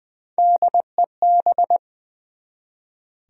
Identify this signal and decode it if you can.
Morse code, heard as a keyed tone, about 20 wpm, 700 Hz. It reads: DEB
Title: USA Common Names - Max 3 Letters - 20wpm